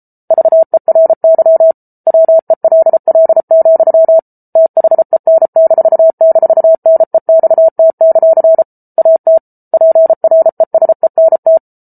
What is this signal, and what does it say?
Morse code, heard as a keyed tone, about 34 wpm, 650 Hz. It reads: VERY WELL, THEN--NEXT; AT PRESENT